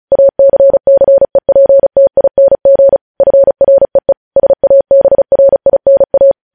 Morse code, heard as a keyed tone, about 36 wpm, 550 Hz. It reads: ACCEPTING FREE SABRINA
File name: podcast / News headlines-1x-35x35